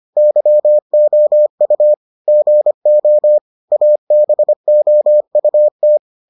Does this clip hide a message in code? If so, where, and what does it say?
Yes: Morse code from 0.2 to 6.0 s, heard as a keyed tone, about 25 wpm, 600 Hz. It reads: YOU GO ABOUT